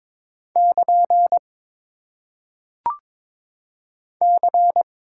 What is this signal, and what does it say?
Morse code, heard as a keyed tone, about 23 wpm, 700 Hz. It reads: <KN> C
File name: Top 90 QSO Elements - 6x Character Spacing - 22wpm